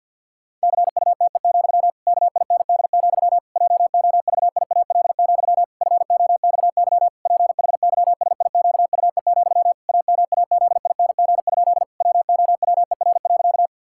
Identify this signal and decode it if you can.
Morse code, heard as a keyed tone, about 50 wpm, 700 Hz. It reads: CUTE, KIND, JOVIAL, FOXY PHYSIQUE, AMAZING? WOWER.
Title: Pangram Challenge v2 - 50wpm